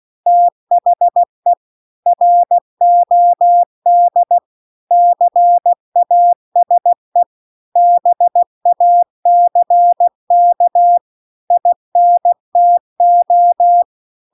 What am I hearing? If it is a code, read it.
Morse code, heard as a keyed tone, about 16 wpm, 700 Hz. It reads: THE ROD CASE BACK INTO
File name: CW-The-Big-Twoheaded-River-2-16WPM